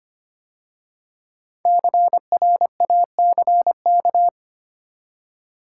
Morse code, heard as a keyed tone, about 25 wpm, 700 Hz. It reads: CRACK
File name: 5 Letter Words - First 3 Letters Given - 25wpm